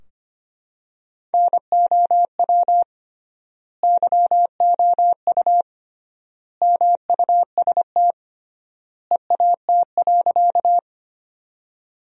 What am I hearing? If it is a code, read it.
Morse code, heard as a keyed tone, about 25 wpm, 700 Hz. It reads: NOW YOU MUST EAT.
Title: 3x Word-Spacing - Sentences from Top 300 Words - 25wpm